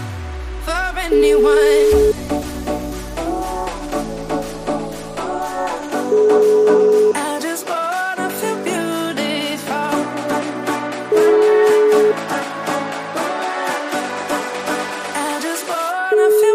Ringback tone (Europe, 425 Hz), with noise.